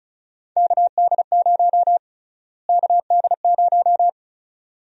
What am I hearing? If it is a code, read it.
Morse code, heard as a keyed tone, about 34 wpm, 700 Hz. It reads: KD0 KD0
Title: USA Call Sign Prefixes - Repeated 2x - 35wpm